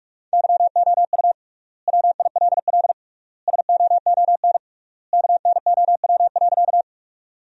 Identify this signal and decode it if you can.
Morse code, heard as a keyed tone, about 41 wpm, 700 Hz. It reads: YOU WILL SOON KNOW.
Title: 3x Word-Spacing - Sentences from Top 400 Words - 45wpm